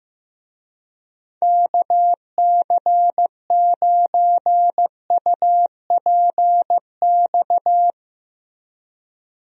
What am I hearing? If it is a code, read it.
Morse code, heard as a keyed tone, about 15 wpm, 700 Hz. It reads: KC9UPX
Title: USA Call Signs - 6 Characters - Repeated 2x - 15wpm